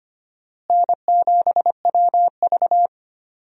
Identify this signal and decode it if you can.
Morse code, heard as a keyed tone, about 25 wpm, 700 Hz. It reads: N7WV
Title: USA Call Signs - 4 Characters - Repeated 2x - 25wpm